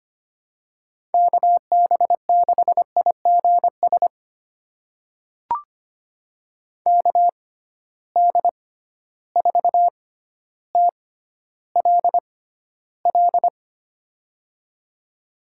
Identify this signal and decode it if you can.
Morse code, heard as a keyed tone, about 25 wpm, spaced as for 13 wpm, 700 Hz. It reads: KB6IGS KD4TLL
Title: USA Call Signs - 6 Characters - 6x Character Spacing - 25wpm